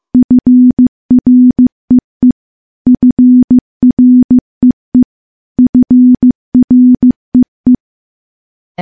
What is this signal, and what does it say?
Morse code, heard as a keyed tone, about 15 wpm, 250 Hz. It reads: FREE FREE FREE
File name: morse-training-15WPM